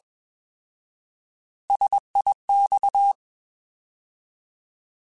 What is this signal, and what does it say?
Morse code, heard as a keyed tone, about 21 wpm, 800 Hz. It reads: SIX
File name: Alphabetical / Alphabetical mp3s / Ss @20